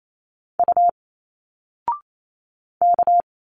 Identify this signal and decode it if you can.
Morse code, heard as a keyed tone, about 28 wpm, 700 Hz. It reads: U K
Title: Single Letters 28wpm